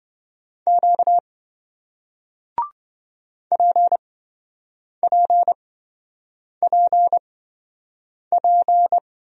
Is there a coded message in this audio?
Yes: Morse code from 0.7 to 9.0 s, heard as a keyed tone, about 27 wpm, spaced as for 7 wpm, 700 Hz. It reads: Q PPPP